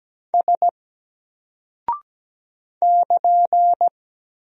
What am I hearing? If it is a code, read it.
Morse code, heard as a keyed tone, about 17 wpm, 700 Hz. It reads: S <KN>
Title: Single Character (Letter:Number:Punctuation:Prosign) Alt 17wpm